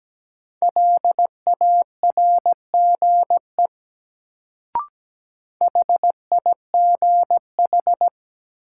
Morse code, heard as a keyed tone, about 17 wpm, 700 Hz. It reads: LARGE HIGH
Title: Sentences from Top 200 Words - New Word Drill - 17wpm